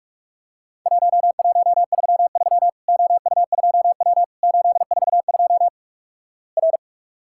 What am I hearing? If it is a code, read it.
Morse code, heard as a keyed tone, about 44 wpm, 700 Hz. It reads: 1133 OU2W 842 R
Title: Worked All Europe DX Contest - QTC Exchange - 45wpm